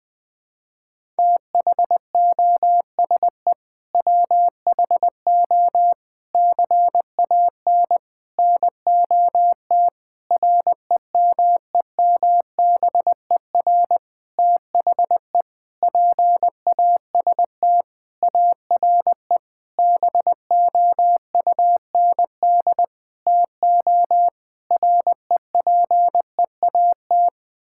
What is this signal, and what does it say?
Morse code, heard as a keyed tone, about 20 wpm, 700 Hz. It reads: THOSE WHO CAN NOT REMEMBER THE PAST ARE BOUND TO REPEAT
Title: Sentences with 8-Word Phrases - Spoken Beginning - 20wpm